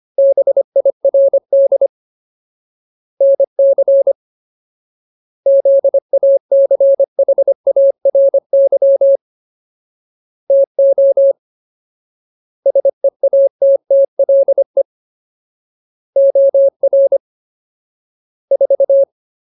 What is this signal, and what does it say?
Morse code, heard as a keyed tone, about 25 wpm, 550 Hz. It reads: BIRD NC ZACHARY TO SEATTLE OR 4
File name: Ultimate Word List / 25WPM Ultimate